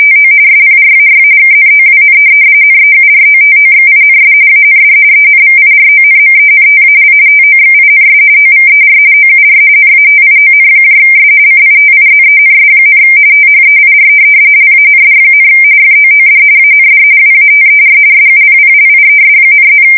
RTTY, about 45.45 baud. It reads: TO SPRAY KNOWN NEUROTOXINS AND ENVIRONMENTAL TOXINS OVER OUR NATIONS CITIZENS, THEIR LAND, FOOD AND WATER SUPPLIES,+
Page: 170 Shift 2125/2295